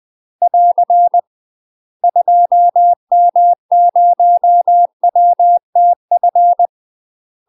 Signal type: Morse code